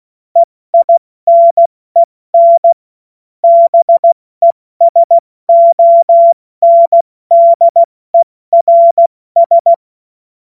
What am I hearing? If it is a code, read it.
Morse code, heard as a keyed tone, about 16 wpm, 680 Hz. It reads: EINEN BESONDERS
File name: Morsekurs / Witze / Witz 05 März 2024 70 F80